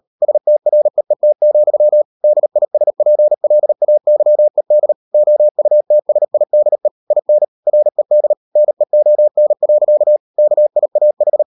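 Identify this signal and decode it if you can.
Morse code, heard as a keyed tone, about 38 wpm, 600 Hz. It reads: STREET, DISPLAYED OUTSIDE IN RED NEON. KIAH
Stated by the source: Morse code practice at 38 WPM, sending Ken Follett's novel Never in English, from CW med Gnister.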